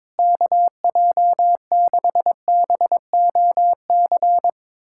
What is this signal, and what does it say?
Morse code, heard as a keyed tone, about 22 wpm, 700 Hz. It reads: KJ6BOC
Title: USA Callsign DE Callsign - 6 Characters - 22wpm